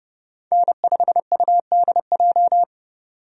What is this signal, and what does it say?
Morse code, heard as a keyed tone, about 32 wpm, 700 Hz. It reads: N5UDJ
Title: USA Call Signs - 5 Characters - 30wpm